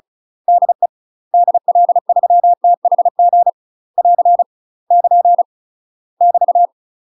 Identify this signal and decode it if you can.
Morse code, heard as a keyed tone, about 36 wpm, 700 Hz. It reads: DE DL3THG <AR> <KN> <BT>
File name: UK MASTER QSOs TRAINING / 35WPM / UK MASTER QSOs 0015